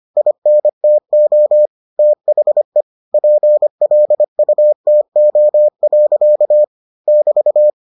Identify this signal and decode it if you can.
Morse code, heard as a keyed tone, about 25 wpm, 600 Hz. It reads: INTO THE PLUTO. <BT>